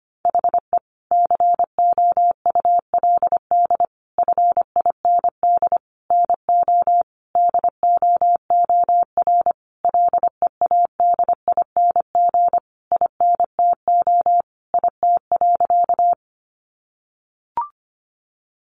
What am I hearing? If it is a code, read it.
Morse code, heard as a keyed tone, about 25 wpm, 700 Hz. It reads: HE COULD FIND NO DOOR LEADING INTO IT.
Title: Sentences from Top 1100 Words 25wpm